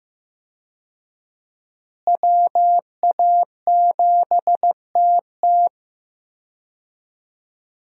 Morse code, heard as a keyed tone, about 15 wpm, 700 Hz. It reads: WA7TT
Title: USA Call Signs - 5 Characters - Repeated 3x - 15wpm